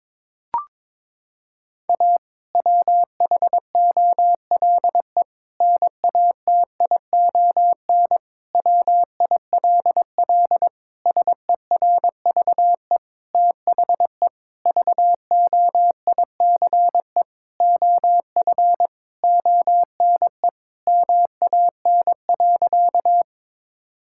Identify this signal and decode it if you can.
Morse code, heard as a keyed tone, about 21 wpm, 700 Hz. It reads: A WHOLE NATION WILL SERVE THE VOICE OF ONE MAN.
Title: Sentences from Top 1000 Words - Review Part 6 - 22wpm